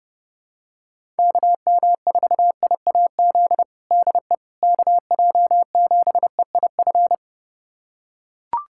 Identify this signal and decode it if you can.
Morse code, heard as a keyed tone, about 32 wpm, 700 Hz. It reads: KM4IAZ DE KJ7EIF
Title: USA Callsign DE Callsign 30wpm